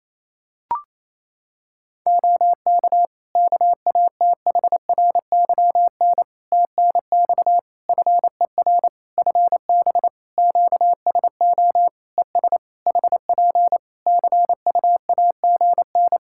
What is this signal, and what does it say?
Morse code, heard as a keyed tone, about 27 wpm, 700 Hz. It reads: OK KATHRYN TNX FER FB QSO ES HP CUAGN
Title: Practice QSO - Exchange 4 - Ending - 28wpm